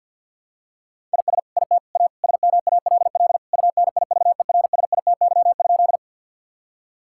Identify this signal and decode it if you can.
Morse code, heard as a keyed tone, about 46 wpm, 700 Hz. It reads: IS IT A SMALL UNIVERSITY?
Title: Sentences from Top 1000 Words - Review Part 5 - 50wpm